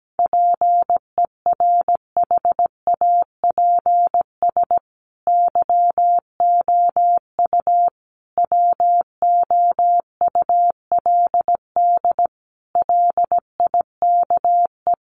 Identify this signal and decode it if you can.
Morse code, heard as a keyed tone, about 17 wpm, 700 Hz. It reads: PERHAPS YOU WOULD LIKE
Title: Sentences from Top 500 Words 17wpm